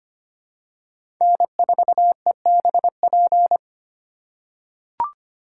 Morse code, heard as a keyed tone, about 25 wpm, 700 Hz. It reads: N4EBP
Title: USA Call Signs - 4x Character Spacing - 25wpm